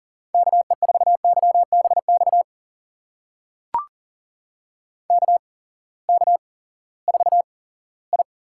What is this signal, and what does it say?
Morse code, heard as a keyed tone, about 38 wpm, spaced as for 20 wpm, 700 Hz. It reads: KE4YBX KK4I
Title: USA Call Signs - 8x Character Spacing - 40wpm